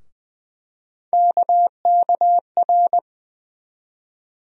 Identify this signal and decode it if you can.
Morse code, heard as a keyed tone, about 20 wpm, 700 Hz. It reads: KKR
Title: USA Call Sign Postfixes - 3 Letters 20wpm